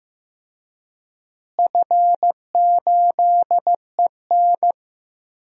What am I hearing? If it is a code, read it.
Morse code, heard as a keyed tone, about 15 wpm, 700 Hz. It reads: F8EN
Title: DXCC-100 Call Signs - 4 Characters - Repeated 2x - 15wpm